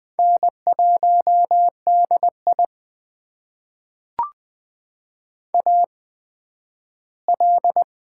Morse code, heard as a keyed tone, about 20 wpm, spaced as for 12 wpm, 700 Hz. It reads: N1DI AL
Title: USA Call Signs - 4 Characters - 8x Character Spacing - 20wpm